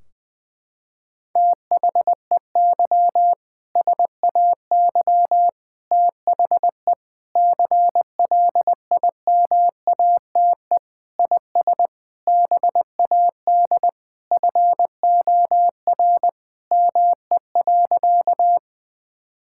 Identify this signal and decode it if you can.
Morse code, heard as a keyed tone, about 20 wpm, 700 Hz. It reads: THEY SAY THE CLIMATE IS BAD FOR ME.
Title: Sentences from Top 1600 Words 20wpm